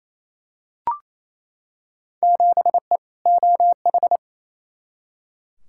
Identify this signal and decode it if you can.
Morse code, heard as a keyed tone, about 27 wpm, 700 Hz. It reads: 7E OH
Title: Field Day Exchanges Encore - 28wpm